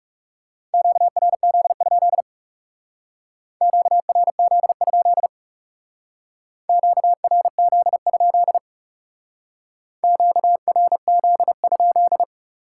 Morse code, heard as a keyed tone, about 38 wpm, 700 Hz. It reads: QRZ? QRZ? QRZ? QRZ?